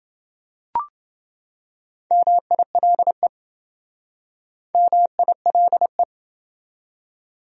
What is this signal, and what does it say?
Morse code, heard as a keyed tone, about 27 wpm, 700 Hz. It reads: MILE MILE